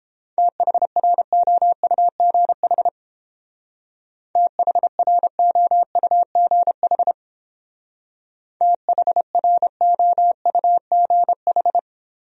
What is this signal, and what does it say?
Morse code, heard as a keyed tone, about 29 wpm, 700 Hz. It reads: THROUGH THROUGH THROUGH